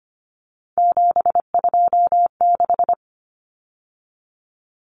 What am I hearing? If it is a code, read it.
Morse code, heard as a keyed tone, about 25 wpm, 700 Hz. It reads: 726
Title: Numbers - 3-Digit 25wpm